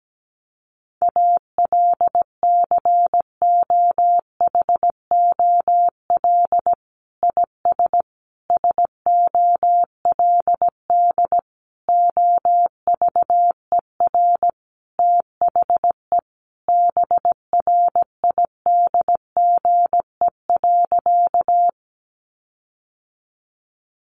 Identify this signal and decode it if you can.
Morse code, heard as a keyed tone, about 17 wpm, 700 Hz. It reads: ALCOHOL IS SOLD OVER THE BRIDGE.
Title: Sentences from Top 1400 Words 17wpm